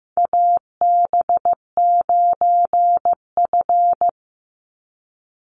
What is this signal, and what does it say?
Morse code, heard as a keyed tone, about 15 wpm, 700 Hz. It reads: AB9F
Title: USA Call Signs 15wpm